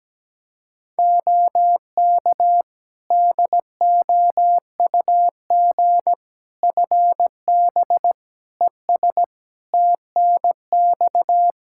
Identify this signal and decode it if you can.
Morse code, heard as a keyed tone, about 17 wpm, 700 Hz. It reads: OK DOUG FB ES TNX
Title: Practice QSO - Exchange 2 - Info - 17wpm